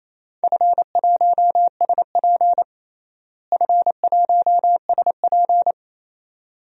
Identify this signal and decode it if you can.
Morse code, heard as a keyed tone, about 28 wpm, 700 Hz. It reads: F1SP F1SP
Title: DXCC-100 Call Signs - 4 Characters - Repeated 2x - 28wpm